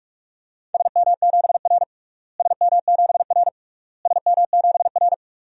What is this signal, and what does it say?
Morse code, heard as a keyed tone, about 41 wpm, 700 Hz. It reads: SM7R SM7R SM7R
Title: DXCC-100 Call Signs - 4 Characters - Repeated 3x - 45wpm